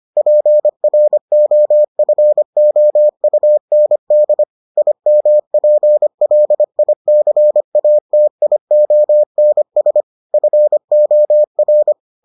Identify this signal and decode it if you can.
Morse code, heard as a keyed tone, about 25 wpm, 600 Hz. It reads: PROFOUND IMPLICATIONS FOR